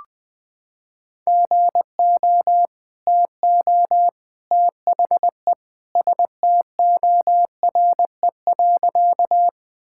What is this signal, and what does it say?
Morse code, heard as a keyed tone, about 21 wpm, 700 Hz. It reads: GO TO THE STORE.
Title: Sentences with 2-Word Phrases - 20wpm